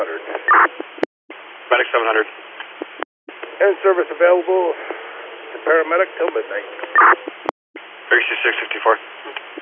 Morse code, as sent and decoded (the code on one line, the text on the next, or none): none